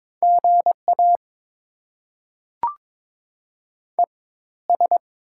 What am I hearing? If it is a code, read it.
Morse code, heard as a keyed tone, about 22 wpm, spaced as for 14 wpm, 690 Hz. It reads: GA ES
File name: Top 20 QSO Elements - 4x Character Spacing - 22wpm